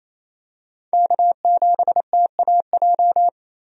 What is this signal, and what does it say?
Morse code, heard as a keyed tone, about 28 wpm, 700 Hz. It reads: K7TAJ
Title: USA Call Signs - 5 Characters - 28wpm